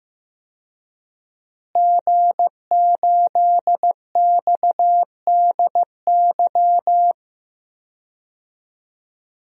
Morse code, heard as a keyed tone, about 15 wpm, 700 Hz. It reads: G8XDY
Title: DXCC-100 Call Signs - 5 Characters - Repeated 3x - 15wpm